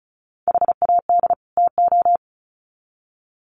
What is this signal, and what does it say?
Morse code, heard as a keyed tone, about 34 wpm, 700 Hz. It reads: HAD TO
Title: Sets of 2 Words 35wpm